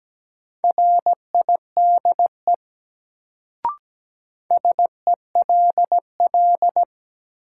Morse code, heard as a keyed tone, about 17 wpm, 700 Hz. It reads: RIDE SELL
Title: Sentences from Top 1000 Words - Review Part 6 - New Word Drill - 17wpm